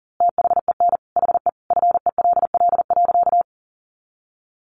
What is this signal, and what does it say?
Morse code, heard as a keyed tone, about 37 wpm, 700 Hz. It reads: THEN HE FELL.
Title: A Sound Of Thunder 40wpm